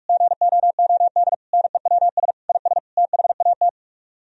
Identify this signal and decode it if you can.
Morse code, heard as a keyed tone, about 49 wpm, 700 Hz. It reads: GOOD NEWS IS THAT